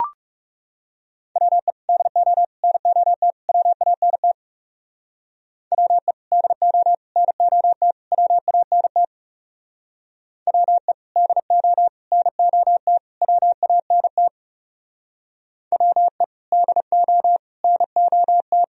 Morse code, heard as a keyed tone, about 37 wpm, 700 Hz. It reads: WE DO NOT WANT WE DO NOT WANT WE DO NOT WANT WE DO NOT
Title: Speed-Racing - Sets of 4 Words 30wpm